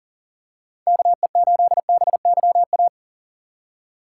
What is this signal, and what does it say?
Morse code, heard as a keyed tone, about 40 wpm, 700 Hz. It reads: KE8BYA
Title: USA Call Signs - 6 Characters - 6x Character Spacing - 40wpm